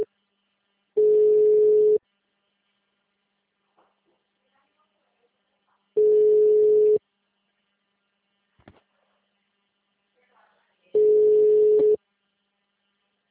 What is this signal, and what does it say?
Ringback tone (Europe, 425 Hz), with speech.